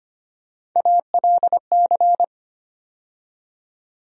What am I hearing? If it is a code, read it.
Morse code, heard as a keyed tone, about 25 wpm, 700 Hz. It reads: ALC
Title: USA Call Sign Postfixes - 3 Letters 25wpm